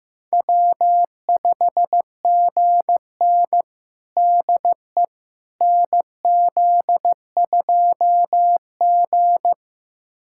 Morse code, heard as a keyed tone, about 15 wpm, 700 Hz. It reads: W5GN DE NZ2G
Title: USA Callsign DE Callsign - 4 Characters - 15wpm